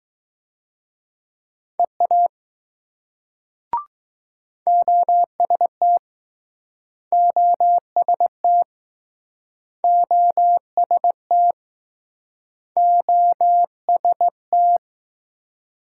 Morse code, heard as a keyed tone, about 20 wpm, 700 Hz. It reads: EA OST OST OST OST